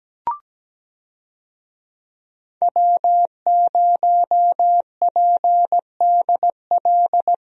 Morse code, heard as a keyed tone, about 17 wpm, 700 Hz. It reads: W0PDL
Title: USA Call Signs - 5 Characters - Repeated 3x - 17wpm